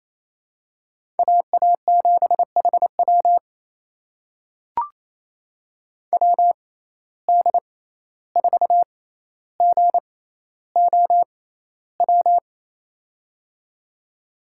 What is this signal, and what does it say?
Morse code, heard as a keyed tone, about 28 wpm, spaced as for 14 wpm, 700 Hz. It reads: AA7HW WD4GOW
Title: USA Call Signs - 6x Character Spacing - 28wpm